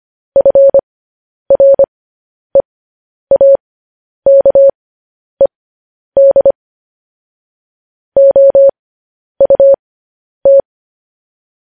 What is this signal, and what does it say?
Morse code, heard as a keyed tone, about 25 wpm, spaced as for 10 wpm, 550 Hz. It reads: FREAKED OUT
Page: News Headlines in Morse Code at 10 WPM